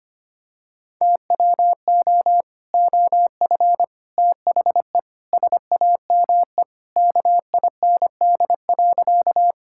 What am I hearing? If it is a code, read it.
Morse code, heard as a keyed tone, about 25 wpm, 700 Hz. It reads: TWO OF THE SAME KIND.
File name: practice / 25wpm / Sentences with 3-Word Phrases - 25wpm